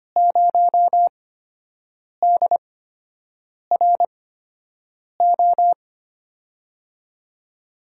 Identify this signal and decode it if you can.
Morse code, heard as a keyed tone, about 25 wpm, spaced as for 7 wpm, 700 Hz. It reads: 0DRO
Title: USA Call Signs - 6 Characters - 8x Character Spacing - 25wpm